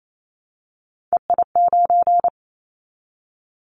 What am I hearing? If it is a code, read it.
Morse code, heard as a keyed tone, about 28 wpm, 700 Hz. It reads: EI9
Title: Callsign Prefixes for Top 50 DXCC - 28wpm